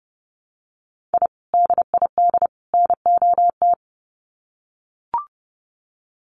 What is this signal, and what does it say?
Morse code, heard as a keyed tone, about 30 wpm, 700 Hz. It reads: I DID NOT
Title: Sets of 3 Words 30wpm